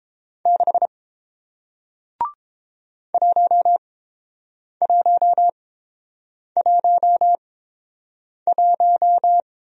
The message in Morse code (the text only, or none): 6 1111